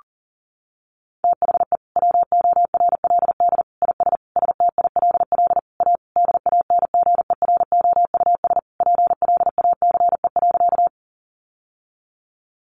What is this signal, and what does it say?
Morse code, heard as a keyed tone, about 40 wpm, 700 Hz. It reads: THE WORLD IS STILL A DANGEROUS PLACE.
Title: Sentences from Top 1200 Words 40wpm